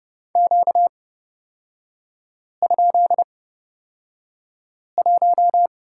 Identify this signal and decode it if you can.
Morse code, heard as a keyed tone, about 30 wpm, spaced as for 5 wpm, 700 Hz. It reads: Q?1
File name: Lesson 38e - Letter q - Three Characters 30wpm